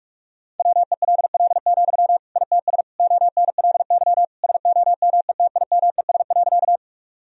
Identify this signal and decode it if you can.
Morse code, heard as a keyed tone, about 46 wpm, 700 Hz. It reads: WELL, ITS ONLY SOMETIMES.
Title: Sentences from Top 1000 Words - Review Part 3 - 45wpm